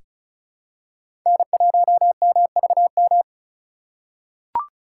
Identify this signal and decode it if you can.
Morse code, heard as a keyed tone, about 36 wpm, 700 Hz. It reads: N1MVM
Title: USA Call Signs - 5 Characters - 2x Character Spacing - 35wpm